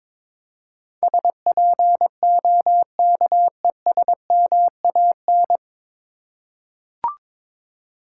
Morse code, heard as a keyed tone, about 21 wpm, 700 Hz. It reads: SPOKESMAN
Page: Morse Code Ninja practice 22wpm